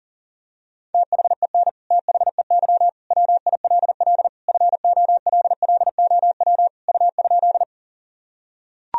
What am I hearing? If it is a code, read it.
Morse code, heard as a keyed tone, about 40 wpm, 700 Hz. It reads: THEN THEY WILL FOLLOW U?
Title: Sentences from Top 200 Words and Abbreviations 40wpm